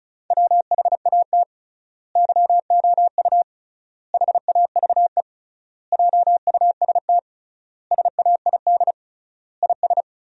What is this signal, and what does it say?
Morse code, heard as a keyed tone, about 34 wpm, 700 Hz. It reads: WHAT YOU HAVE JUST SAID IS
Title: 3x Word-Spacing - Sentences from Top 200 Words - 35wpm